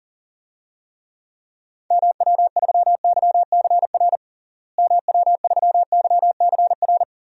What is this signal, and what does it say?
Morse code, heard as a keyed tone, about 40 wpm, 700 Hz. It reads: MW3YCR MW3YCR
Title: DXCC-100 Call Signs - 6 Characters - Repeated 3x - 40wpm